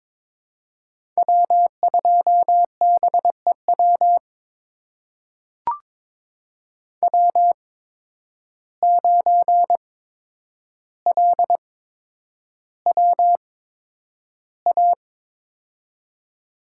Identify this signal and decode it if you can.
Morse code, heard as a keyed tone, about 22 wpm, spaced as for 9 wpm, 700 Hz. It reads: W2BEW W9LWA
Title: USA Call Signs - 5 Characters - 8x Character Spacing - 22wpm